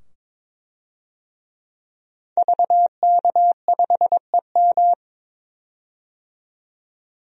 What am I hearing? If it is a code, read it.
Morse code, heard as a keyed tone, about 22 wpm, 700 Hz. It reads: VK5EM
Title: DXCC-100 Call Signs - 5 Characters - 22wpm